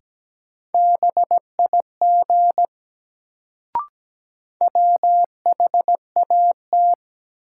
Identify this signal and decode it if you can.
Morse code, heard as a keyed tone, about 17 wpm, 700 Hz. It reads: BIG WHAT
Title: Sets of 2 Words - New Word Drill - 17wpm